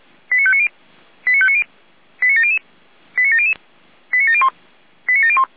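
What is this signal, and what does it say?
Five-tone selective call (ZVEI), heard as quick runs of five tones, tones of about 70 ms. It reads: 89590 89590 89700 89700 89701 89701
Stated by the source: ZVEI five-tone selective call 89020, received by SDR.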